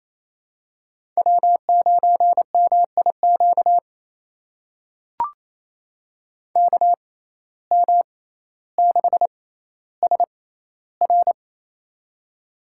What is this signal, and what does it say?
Morse code, heard as a keyed tone, about 28 wpm, spaced as for 14 wpm, 700 Hz. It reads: W9MIQ KM6SR
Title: USA Call Signs - 5 Characters - 6x Character Spacing - 28wpm